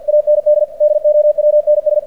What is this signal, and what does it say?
Morse code, heard as a keyed tone, about 46 wpm, 600 Hz. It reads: AND DOWNS
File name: archived / CRSnet